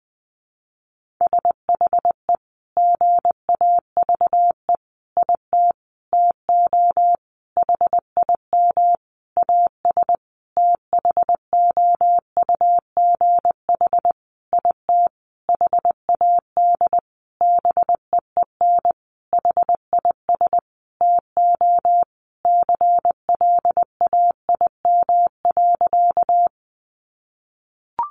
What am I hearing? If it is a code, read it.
Morse code, heard as a keyed tone, about 20 wpm, 700 Hz. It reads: SHE GAVE IT TO HIM AS THOUGH IT HAD BEEN HIS TO CLAIM.
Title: Sentences from Top 1500 Words 20wpm